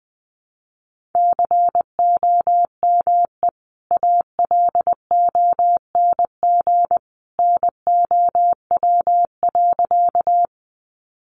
Morse code, heard as a keyed tone, about 20 wpm, 700 Hz. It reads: COME ALONG NOW.
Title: Sentences from Top 700 Words 20wpm